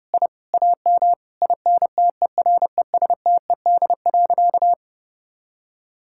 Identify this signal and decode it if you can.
Morse code, heard as a keyed tone, about 30 wpm, 700 Hz. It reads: I AM INTERESTED.
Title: Sentences from Top 1000 Words - Review Part 6 - 30wpm